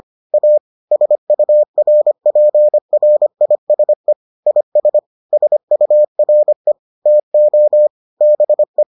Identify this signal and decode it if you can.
Morse code, heard as a keyed tone, about 25 wpm, 600 Hz. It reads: A SURPRISE IS SURE TO BE